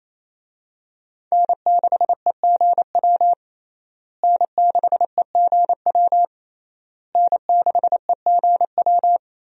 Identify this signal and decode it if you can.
Morse code, heard as a keyed tone, about 28 wpm, 700 Hz. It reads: N6EGW N6EGW N6EGW
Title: USA Call Signs - 5 Characters - Repeated 3x - 28wpm